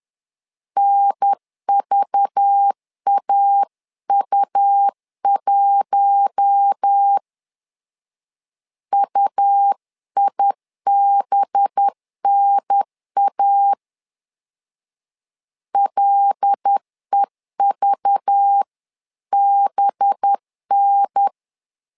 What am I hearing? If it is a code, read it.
Morse code, heard as a keyed tone, about 10 wpm, 800 Hz. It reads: NVAU1 UIBNA LEVBN
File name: LEZIONI / 5 LBVC / 5 LBV 10wpm 2248